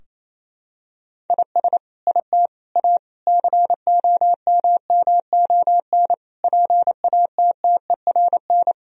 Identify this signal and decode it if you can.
Morse code, heard as a keyed tone, about 28 wpm, 700 Hz. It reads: IS IT A COMMON PATTERN
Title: Sentences from Top 1000 Words - Review Part 2 - 28wpm